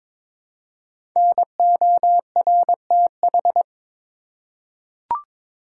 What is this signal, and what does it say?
Morse code, heard as a keyed tone, about 22 wpm, 700 Hz. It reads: NORTH
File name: 5 Letter Words - First 3 Letters Given - 22wpm